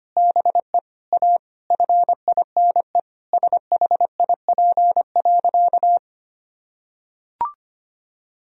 Morse code, heard as a keyed tone, about 25 wpm, 700 Hz. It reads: BE A FINE SHIP.
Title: Sentences from Top 1000 Words - Review Part 2 - 25wpm